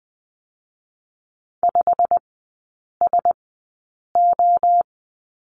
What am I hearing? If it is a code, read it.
Morse code, heard as a keyed tone, about 20 wpm, spaced as for 8 wpm, 700 Hz. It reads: 5SO
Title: 100 Lesson 17e - Letter u - Three Characters 20wpm